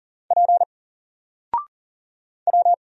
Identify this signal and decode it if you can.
Morse code, heard as a keyed tone, about 40 wpm, 700 Hz. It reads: P W
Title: Single Letter-Number Alt 40wpm